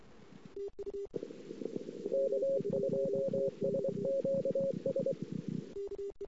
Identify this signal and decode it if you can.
Morse code, heard as a keyed tone, about 24 wpm, 540 Hz. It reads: K2SQS
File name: CW / 2008-ss-cw-k5tr-op-n5rz -22:00